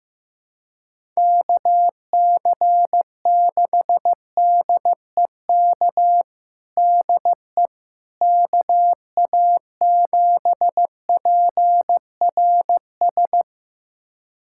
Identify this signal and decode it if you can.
Morse code, heard as a keyed tone, about 15 wpm, 700 Hz. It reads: KC6DEK DE KA7PRS
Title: USA Callsign DE Callsign - 6 Characters - 15wpm